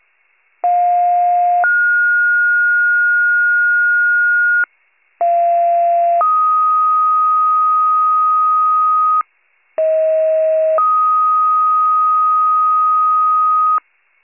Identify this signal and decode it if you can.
Two-tone page, heard as a one-second tone, then a three-second tone, about 710 then 1430 Hz, about 690 then 1250 Hz, about 640 then 1250 Hz. It is heard from 0.6 to 13.8 s.